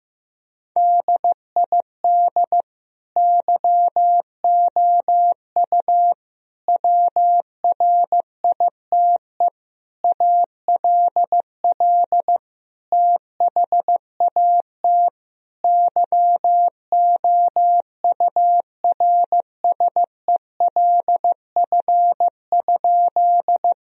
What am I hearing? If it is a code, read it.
Morse code, heard as a keyed tone, about 16 wpm, 700 Hz. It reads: DID YOU WRITE ALL THAT YOURSELF?
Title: Sentences from Top 1000 Words - Review Part 1 - 15wpm